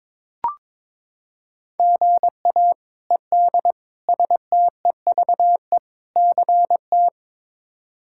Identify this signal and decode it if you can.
Morse code, heard as a keyed tone, about 22 wpm, 700 Hz. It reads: GA ED STEVE CT
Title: K1USN SST QSO Practice - 22wpm